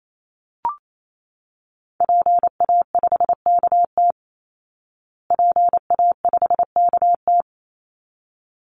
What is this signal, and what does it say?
Morse code, heard as a keyed tone, about 28 wpm, 700 Hz. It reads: PA5KT PA5KT
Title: Call Signs Repeated 3x - 28wpm